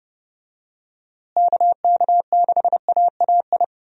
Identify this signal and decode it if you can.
Morse code, heard as a keyed tone, about 30 wpm, 700 Hz. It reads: KK6AAI
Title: USA Call Signs - 6 Characters - Repeated 3x - 30wpm